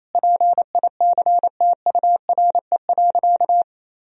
Morse code, heard as a keyed tone, about 28 wpm, 700 Hz. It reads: PICTURE.
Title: Sentences from Top 1000 Words - Review Part 1 - 28wpm